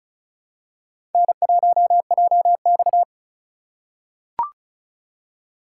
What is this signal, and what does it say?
Morse code, heard as a keyed tone, about 34 wpm, 700 Hz. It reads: N1JX
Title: USA Call Signs - 4 Characters - 4x Character Spacing - 35wpm